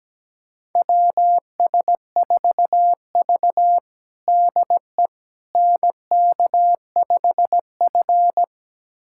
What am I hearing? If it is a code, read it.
Morse code, heard as a keyed tone, about 17 wpm, 700 Hz. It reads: WS4V DE NK5F
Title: USA Callsign DE Callsign - 4 Characters - 17wpm